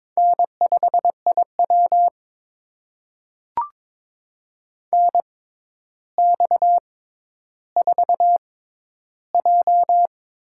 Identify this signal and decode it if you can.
Morse code, heard as a keyed tone, about 22 wpm, spaced as for 11 wpm, 700 Hz. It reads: N5IW NX4J
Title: USA Call Signs - 4 Characters - 6x Character Spacing - 22wpm